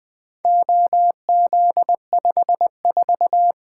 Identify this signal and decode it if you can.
Morse code, heard as a keyed tone, about 20 wpm, 700 Hz. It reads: OZ54
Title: Four Letter-Number Combinations - 20wpm